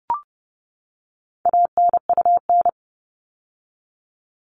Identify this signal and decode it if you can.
Morse code, heard as a keyed tone, about 30 wpm, 700 Hz. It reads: ANUN
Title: All Cut Numbers - 4-Digit 30wpm